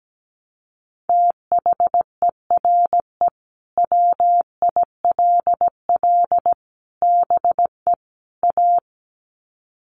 Morse code, heard as a keyed tone, about 17 wpm, 700 Hz. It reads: THERE WILL BE A
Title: Sets of 4 Words 17wpm